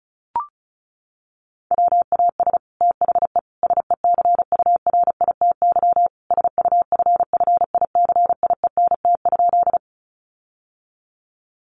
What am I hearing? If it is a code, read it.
Morse code, heard as a keyed tone, about 34 wpm, 700 Hz. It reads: WAS THE SECURITY SUFFICIENT?
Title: Sentences from Top 1400 Words 35wpm